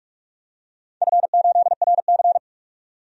Morse code, heard as a keyed tone, about 47 wpm, 700 Hz. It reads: F8RC
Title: DXCC-100 Call Signs - 4 Characters - 45wpm